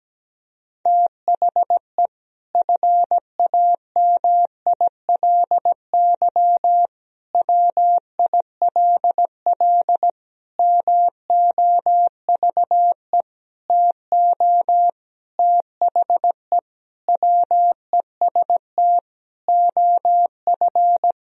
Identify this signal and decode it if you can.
Morse code, heard as a keyed tone, about 17 wpm, 700 Hz. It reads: THE FAMILY WILL MOVE TO THE WEST OF
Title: Sentences from Top 1000 Words - Review Part 2 - 17wpm